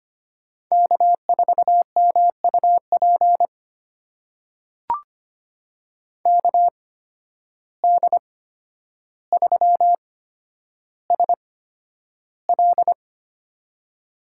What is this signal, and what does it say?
Morse code, heard as a keyed tone, about 25 wpm, spaced as for 11 wpm, 700 Hz. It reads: K4MUP KD3SL
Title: USA Call Signs - 5 Characters - 8x Character Spacing - 25wpm